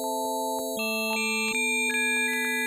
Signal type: none